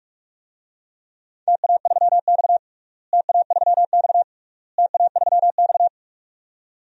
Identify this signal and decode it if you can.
Morse code, heard as a keyed tone, about 47 wpm, 700 Hz. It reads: TA3X TA3X TA3X
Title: DXCC-100 Call Signs - 4 Characters - Repeated 3x - 45wpm